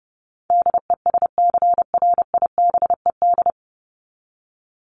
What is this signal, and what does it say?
Morse code, heard as a keyed tone, about 30 wpm, 700 Hz. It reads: DESCRIBED
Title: ED Suffix - Top 100 Words with Base Words 30wpm